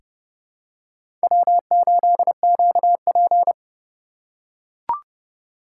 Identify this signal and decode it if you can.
Morse code, heard as a keyed tone, about 29 wpm, 700 Hz. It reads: W8QP
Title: USA Call Signs - 4 Characters - 3x Character Spacing - 30wpm